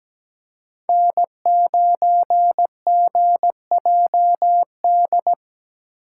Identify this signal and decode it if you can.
Morse code, heard as a keyed tone, about 17 wpm, 700 Hz. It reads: N9GJD
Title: USA Call Signs - 8x Character Spacing - 17wpm